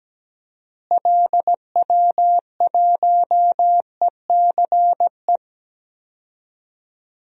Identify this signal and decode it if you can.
Morse code, heard as a keyed tone, about 17 wpm, 700 Hz. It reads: LW1ECE
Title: DXCC-100 Call Signs - 6 Characters - Repeated 3x - 17wpm